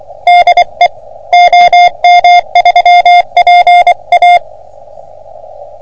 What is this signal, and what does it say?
Morse code, heard as a keyed tone, about 24 wpm, 710 Hz. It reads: DE OM3PA
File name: OM3PA PILUP CUC 388